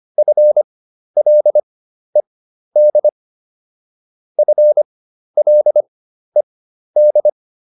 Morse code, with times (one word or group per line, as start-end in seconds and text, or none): 0.2-3.1 FLED
4.4-7.3 FLED